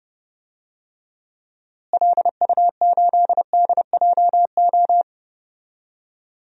Morse code, heard as a keyed tone, about 30 wpm, 700 Hz. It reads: LU8DJO
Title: DXCC-100 Call Signs - 6 Characters - Repeated 3x - 30wpm